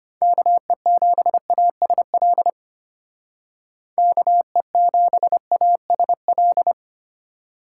Morse code, heard as a keyed tone, about 32 wpm, 700 Hz. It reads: KE7ASL KE7ASL